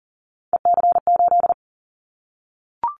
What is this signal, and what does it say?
Morse code, heard as a keyed tone, about 40 wpm, 700 Hz. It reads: EC8
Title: Callsign Prefixes for Top 100 DXCC - 40wpm